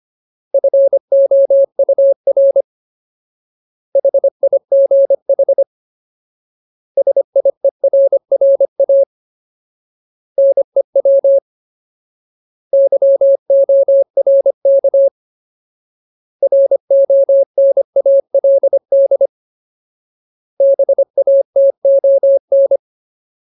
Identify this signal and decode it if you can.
Morse code, heard as a keyed tone, about 25 wpm, 550 Hz. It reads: FOUR HIGH SIERRA NEW YORK RONALD BATON
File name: Ultimate Word List / 25WPM Ultimate